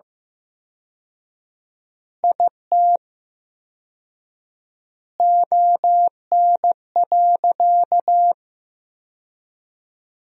Morse code, heard as a keyed tone, about 15 wpm, 700 Hz. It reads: IT ON.
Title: 4x Word-Spacing - Sentences from Top 200 Words - 15wpm